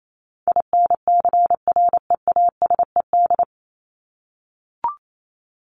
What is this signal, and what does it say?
Morse code, heard as a keyed tone, about 28 wpm, 700 Hz. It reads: INCREASED
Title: ED Suffix - Top 100 Words with Base Words 28wpm